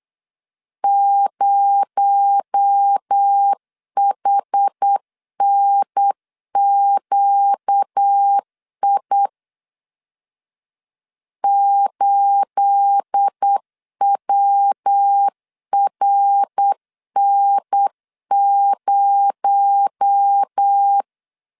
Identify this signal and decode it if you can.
Morse code, heard as a keyed tone, about 8 wpm, 800 Hz. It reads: 0HNQI 8WRN0
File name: CW / DET / 1 9 1500C S B 8wpm 5624